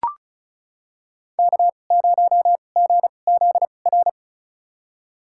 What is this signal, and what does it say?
Morse code, heard as a keyed tone, about 36 wpm, spaced as for 25 wpm, 700 Hz. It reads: K0GZR
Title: USA Call Signs - 5 Characters - 2x Character Spacing - 35wpm